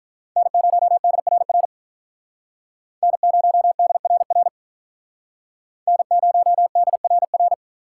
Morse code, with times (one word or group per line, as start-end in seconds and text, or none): 0.4-1.7 N0DRR
3.0-4.5 N0DRR
5.9-7.5 N0DRR